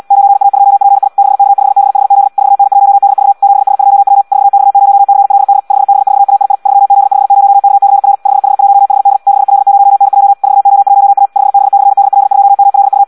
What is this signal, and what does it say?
Morse code, heard as a keyed tone, about 75 wpm, 800 Hz. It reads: 9A2PI 6K5BHZ 4N1DV 7S2W 4X0CLD 5B4AII 3Z50KCR 5H1HD 6L0NJ 4O1A 5B/DL2MLU
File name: Audio practica CW / qrq-tx-78 2